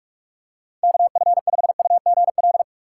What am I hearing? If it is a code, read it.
Morse code, heard as a keyed tone, about 42 wpm, 700 Hz. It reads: KF5UGL
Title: USA Call Signs - 6 Characters - 6x Character Spacing - 45wpm